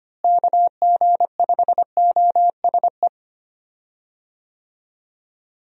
Morse code, heard as a keyed tone, about 25 wpm, 700 Hz. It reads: KG5OSE
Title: USA Call Signs - 6 Characters - Repeated 2x - 25wpm